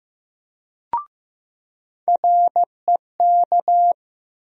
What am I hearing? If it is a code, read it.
Morse code, heard as a keyed tone, about 15 wpm, 700 Hz. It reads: REK
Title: USA Call Sign Postfixes - 3 Letters 15wpm